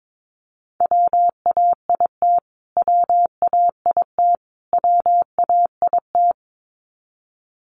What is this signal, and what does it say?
Morse code, heard as a keyed tone, about 22 wpm, 700 Hz. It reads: WAIT WAIT WAIT
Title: Top 500 Words Repeated 22wpm